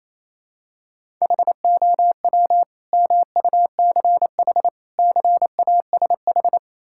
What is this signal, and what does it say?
Morse code, heard as a keyed tone, about 28 wpm, 700 Hz. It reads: HOW MUCH CASH
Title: Sentences from Top 1000 Words - Review Part 5 - 28wpm